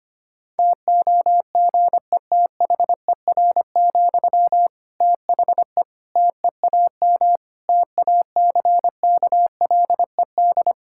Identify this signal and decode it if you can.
Morse code, heard as a keyed tone, about 25 wpm, 700 Hz. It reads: TOGETHER, THE TEAM TACKLED
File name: practice / 25wpm / Difficult Sentences with Hints - 25wpm